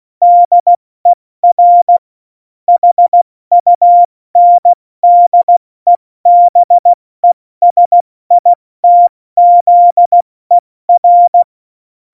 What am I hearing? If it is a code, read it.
Morse code, heard as a keyed tone, about 16 wpm, 700 Hz. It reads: DER HUNDEBESITZER
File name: Morsekurs / Witze / Witz 25. Dezember, 70 F80